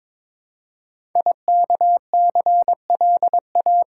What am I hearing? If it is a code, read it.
Morse code, heard as a keyed tone, about 22 wpm, 700 Hz. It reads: IKCLA
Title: Code Groups - 5 Letter:Numbers - 22wpm